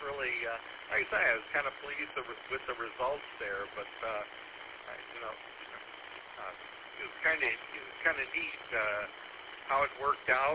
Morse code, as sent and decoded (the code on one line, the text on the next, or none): none